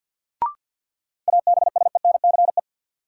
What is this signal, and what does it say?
Morse code, heard as a keyed tone, about 47 wpm, 700 Hz. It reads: ABSENCE
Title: Sets of 4 Words - New Word Drill - 50wpm